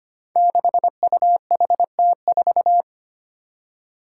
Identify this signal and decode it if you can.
Morse code, heard as a keyed tone, about 25 wpm, 700 Hz. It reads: 6UHT4
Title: Five Challenging Letter-Number Combinations 25wpm